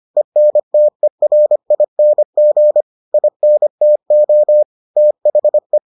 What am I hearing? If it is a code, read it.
Morse code, heard as a keyed tone, about 25 wpm, 600 Hz. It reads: ENTERING INTO THE